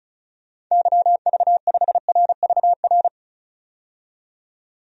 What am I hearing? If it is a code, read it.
Morse code, heard as a keyed tone, about 36 wpm, 700 Hz. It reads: YV5RVR
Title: DXCC-100 Call Signs - 6 Characters - 35wpm